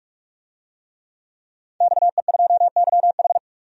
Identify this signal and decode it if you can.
Morse code, heard as a keyed tone, about 42 wpm, 700 Hz. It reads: XE2YH
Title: DXCC-100 Call Signs - 5 Characters - 45wpm